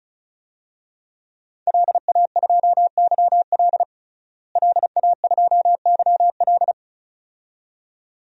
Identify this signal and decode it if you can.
Morse code, heard as a keyed tone, about 36 wpm, 700 Hz. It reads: LA2YL LA2YL
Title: DXCC-100 Call Signs - 5 Characters - Repeated 2x - 35wpm